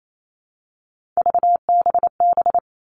Morse code, heard as a keyed tone, about 28 wpm, 700 Hz. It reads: VBB
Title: Three Challenging Letter-Number Combinations 28wpm